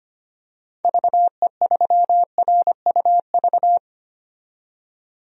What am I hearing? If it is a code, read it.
Morse code, heard as a keyed tone, about 25 wpm, 700 Hz. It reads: VE3RUV
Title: DXCC-100 Call Signs - 6 Characters - 25wpm